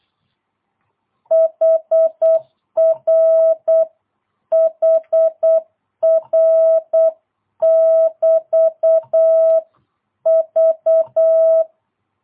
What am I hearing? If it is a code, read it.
Morse code, heard as a keyed tone, about 8 wpm, 650 Hz. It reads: HR HR <BT> V
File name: jh8cbh / 2022cw